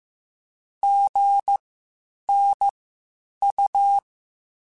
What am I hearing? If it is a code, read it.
Morse code, heard as a keyed tone, about 15 wpm, spaced as for 8 wpm, 800 Hz. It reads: GNU
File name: Spaced Alphas / Spaced Alphas mp3s / Gs sp @13